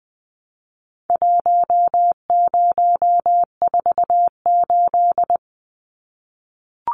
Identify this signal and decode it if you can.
Morse code, heard as a keyed tone, about 20 wpm, 700 Hz. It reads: 1048
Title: Numbers - Contest Style - 4-Digit 20wpm